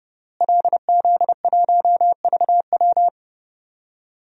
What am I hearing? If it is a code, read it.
Morse code, heard as a keyed tone, about 30 wpm, 700 Hz. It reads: LZ1VW
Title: DXCC-100 Call Signs - 5 Characters - Repeated 3x - 30wpm